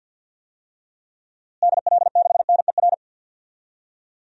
Morse code, heard as a keyed tone, about 50 wpm, 700 Hz. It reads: DL6NER
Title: DXCC-100 Call Signs - 6 Characters - Repeated 3x - 50wpm